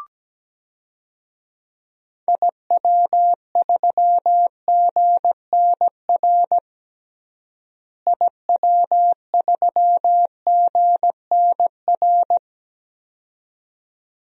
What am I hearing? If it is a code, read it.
Morse code, heard as a keyed tone, about 17 wpm, 700 Hz. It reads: IW3GNR IW3GNR
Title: DXCC-100 Call Signs - 6 Characters - Repeated 2x - 17wpm